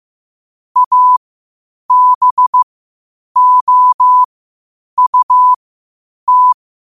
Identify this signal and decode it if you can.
Morse code, heard as a keyed tone, about 15 wpm, spaced as for 8 wpm, 1000 Hz. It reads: ABOUT
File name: qotdfiles / 8wpm